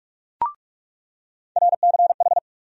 Morse code, heard as a keyed tone, about 46 wpm, 700 Hz. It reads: RCH
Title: USA Call Sign Postfixes - 3 Letters 45wpm